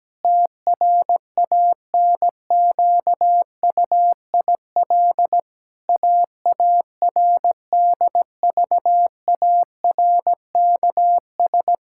Morse code, heard as a keyed tone, about 17 wpm, 700 Hz. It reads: TRANQUIL AARDVARKS
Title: Pangram Challenge v2 - 17wpm